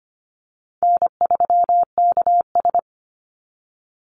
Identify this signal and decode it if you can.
Morse code, heard as a keyed tone, about 25 wpm, 700 Hz. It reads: N3KS
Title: Easy Call Signs Repeated 2x - 25wpm